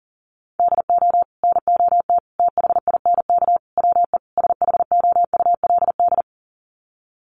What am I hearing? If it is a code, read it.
Morse code, heard as a keyed tone, about 40 wpm, 700 Hz. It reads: DO NOT THINK WE SHOULD
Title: Sets of 5 Words Encore 40wpm